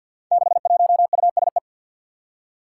Morse code, heard as a keyed tone, about 46 wpm, 700 Hz. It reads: 61USE
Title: Five Challenging Letter-Number Combinations 50wpm